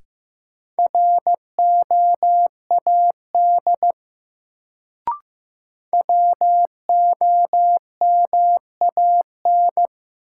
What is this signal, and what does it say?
Morse code, heard as a keyed tone, about 15 wpm, 700 Hz. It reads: ROAD WOMAN
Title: Sets of 5 Words - New Word Drill - 15wpm